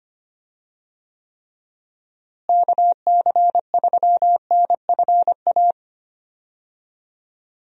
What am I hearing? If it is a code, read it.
Morse code, heard as a keyed tone, about 25 wpm, 700 Hz. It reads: KC3NFA
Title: USA Call Signs - 6 Characters - Repeated 2x - 25wpm